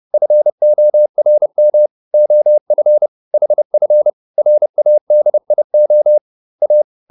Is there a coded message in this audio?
Yes: Morse code from 0.1 to 6.8 s, heard as a keyed tone, about 30 wpm, 600 Hz. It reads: FORM OF HF RADIO A